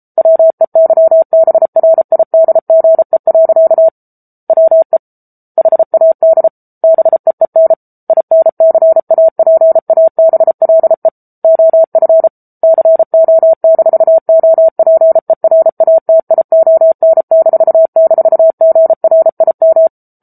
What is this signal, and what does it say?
Morse code, heard as a keyed tone, about 34 wpm, 650 Hz. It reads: WEYBRIDGE. WE HAD BEEN INCAPABLE OF CO-OPERATION--GRIM
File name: Book Two / 07-Chapter 7 - 33 WPM